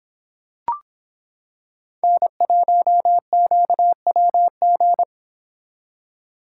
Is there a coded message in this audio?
Yes: Morse code from 2.0 to 5.0 s, heard as a keyed tone, about 25 wpm, 700 Hz. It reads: N1QWG